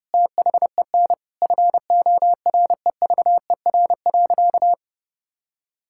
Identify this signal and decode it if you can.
Morse code, heard as a keyed tone, about 30 wpm, 700 Hz. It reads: THEN FOREVER.
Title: Sentences from Top 1600 Words 30wpm